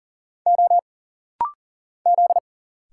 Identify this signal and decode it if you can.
Morse code, heard as a keyed tone, about 40 wpm, 700 Hz. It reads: O Z
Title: Single Letters - Rapid-Fire 40wpm